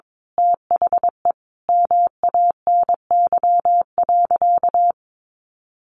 Morse code, heard as a keyed tone, about 22 wpm, 700 Hz. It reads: THE MANY.
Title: Sentences from Top 1300 Words 22wpm